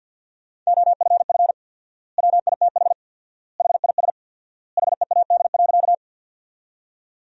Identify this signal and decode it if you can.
Morse code, heard as a keyed tone, about 50 wpm, 700 Hz. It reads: OFF WITH HIS HEAD.
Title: 4x Word-Spacing - Sentences from Top 300 Words - 50wpm